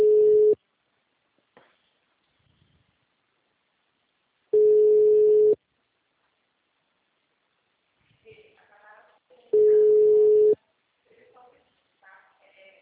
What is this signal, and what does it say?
Ringback tone (Europe, 425 Hz), with speech.